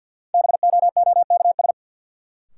Morse code, heard as a keyed tone, about 52 wpm, 700 Hz. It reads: BOOKS
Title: Sentences from Top 1000 Words - Review Part 4 - New Word Drill - 50wpm